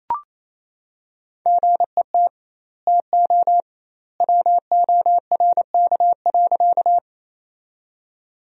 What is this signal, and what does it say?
Morse code, heard as a keyed tone, about 28 wpm, 700 Hz. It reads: GET TO WORK.
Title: 2x Word-Spacing - Sentences from Top 200 Words - 28wpm